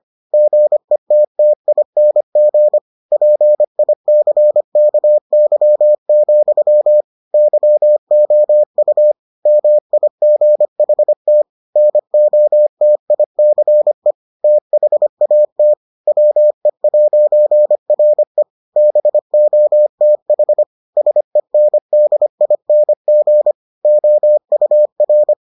Morse code, heard as a keyed tone, about 25 wpm, 600 Hz. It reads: GETTING PICKY, YOU MIGHT NOTICE THAT WE'RE BOTH SENDING OUR